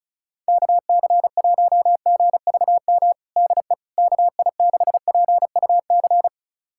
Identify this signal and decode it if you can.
Morse code, heard as a keyed tone, about 36 wpm, 700 Hz. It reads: KC1GVM DE KI6PUC
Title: USA Callsign DE Callsign 35wpm